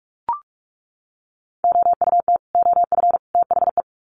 Morse code, heard as a keyed tone, about 44 wpm, 700 Hz. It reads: OUT OF THE
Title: Sets of 3 Words 45wpm